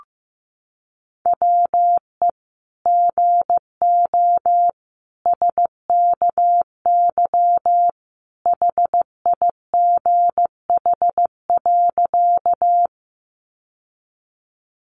Morse code, heard as a keyed tone, about 15 wpm, 700 Hz. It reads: WE GO SKY HIGH.
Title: Sentences from Top 1200 Words 15wpm